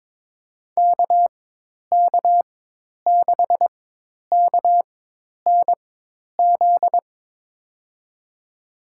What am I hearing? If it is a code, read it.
Morse code, heard as a keyed tone, about 22 wpm, spaced as for 10 wpm, 700 Hz. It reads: KK6KNZ
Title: USA Call Signs - 4x Character Spacing - 22wpm